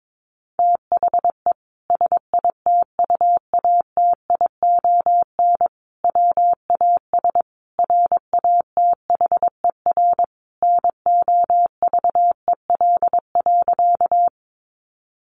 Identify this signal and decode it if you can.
Morse code, heard as a keyed tone, about 22 wpm, 700 Hz. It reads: THE SITUATION WAS RATHER NOVEL.
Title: Sentences from Top 1300 Words 22wpm